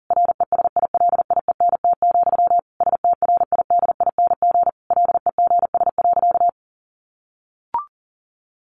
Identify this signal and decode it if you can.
Morse code, heard as a keyed tone, about 40 wpm, 700 Hz. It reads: RESILIENT, STRIDING LEGS.
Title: A Sound Of Thunder 40wpm